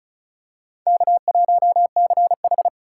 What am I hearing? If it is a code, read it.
Morse code, heard as a keyed tone, about 36 wpm, 700 Hz. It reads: K1CH
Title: USA Call Signs - 4 Characters - 4x Character Spacing - 35wpm